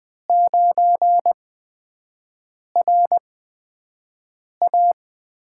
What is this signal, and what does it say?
Morse code, heard as a keyed tone, about 20 wpm, spaced as for 5 wpm, 700 Hz. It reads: 9RA
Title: USA Call Signs - 8x Character Spacing - 20wpm